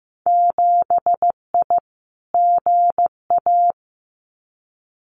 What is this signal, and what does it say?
Morse code, heard as a keyed tone, about 15 wpm, 700 Hz. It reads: 7I GA
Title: Winter Field Day Exchanges Encore 15wpm